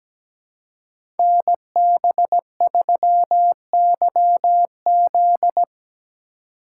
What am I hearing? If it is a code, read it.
Morse code, heard as a keyed tone, about 17 wpm, 700 Hz. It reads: NB3YZ
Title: Code Groups - 5 Letter:Numbers - 17wpm